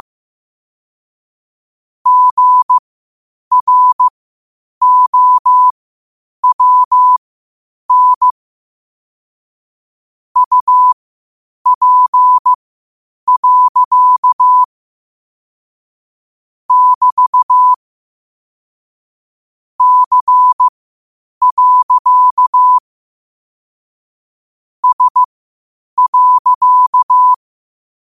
Morse code, heard as a keyed tone, about 15 wpm, spaced as for 8 wpm, 1000 Hz. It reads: GROWN UP. <BT> C. S.
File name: qotdfiles / 8wpm